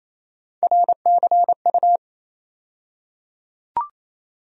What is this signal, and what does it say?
Morse code, heard as a keyed tone, about 27 wpm, 700 Hz. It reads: RCU
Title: USA Call Sign Postfixes - 3 Letters 28wpm